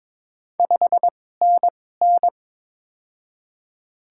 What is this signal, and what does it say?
Morse code, heard as a keyed tone, about 22 wpm, spaced as for 16 wpm, 700 Hz. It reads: 5NN
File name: Top 20 QSO Elements - 2x Character Spacing - 22wpm